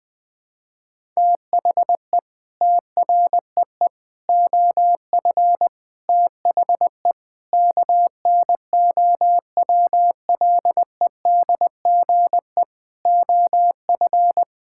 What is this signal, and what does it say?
Morse code, heard as a keyed tone, about 20 wpm, 700 Hz. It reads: THE TREE OF THE KNOWLEDGE OF
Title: Sets of 9 Words - 20wpm